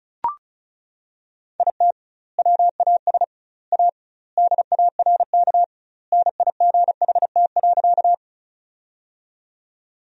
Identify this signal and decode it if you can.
Morse code, heard as a keyed tone, about 36 wpm, 700 Hz. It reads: IT WAS A DARK NIGHT.
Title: 2x Word-Spacing - Sentences from Top 500 Words - 35wpm